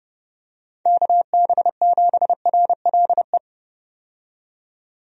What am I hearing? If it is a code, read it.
Morse code, heard as a keyed tone, about 30 wpm, 700 Hz. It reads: KB7RLE
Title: USA Call Signs - 6 Characters - 30wpm